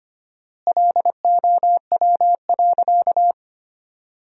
Morse code, heard as a keyed tone, about 25 wpm, 700 Hz. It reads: LOW.
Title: 4x Word-Spacing - Sentences from Top 500 Words - 25wpm